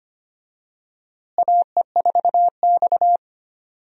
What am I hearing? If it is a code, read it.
Morse code, heard as a keyed tone, about 25 wpm, 700 Hz. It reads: AE4X
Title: USA Call Signs - 4 Characters - Repeated 3x - 25wpm